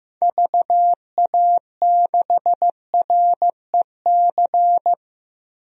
Morse code, heard as a keyed tone, about 15 wpm, 700 Hz. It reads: VA6REC
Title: DXCC-100 Call Signs - 6 Characters - 15wpm